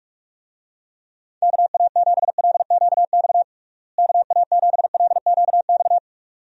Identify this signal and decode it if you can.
Morse code, heard as a keyed tone, about 48 wpm, 700 Hz. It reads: KA7LQX KA7LQX
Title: USA Call Signs - 6 Characters - Repeated 2x - 45wpm